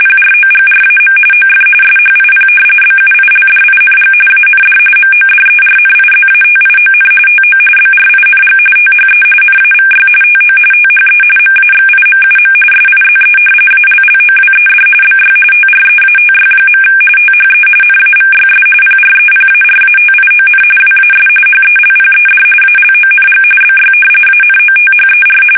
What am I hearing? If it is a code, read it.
RTTY, about 45.45 baud. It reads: PROGRAM OF SHIP BUILDING AND MODERNIZATION TO BRING THE NAVY TO THE MAXIMUM SIZE ALLOWED BY TREATY. THE NAVYS PREPARATION WAS HELPED ALONG BY ANOTHER
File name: fsk850